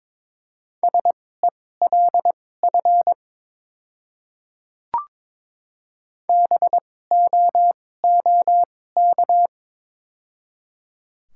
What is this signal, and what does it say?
Morse code, heard as a keyed tone, about 22 wpm, spaced as for 16 wpm, 700 Hz. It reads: SELF BOOK
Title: 2x Character Spacing - Top 300 Words - 22wpm